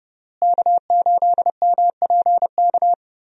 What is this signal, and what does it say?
Morse code, heard as a keyed tone, about 30 wpm, 700 Hz. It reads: K8MPK
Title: USA Call Signs - 5 Characters - Repeated 2x - 30wpm